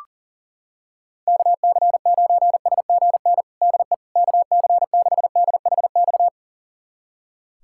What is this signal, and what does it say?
Morse code, heard as a keyed tone, about 40 wpm, 700 Hz. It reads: KC9SGN DE KC6DHX
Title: USA Callsign DE Callsign 40wpm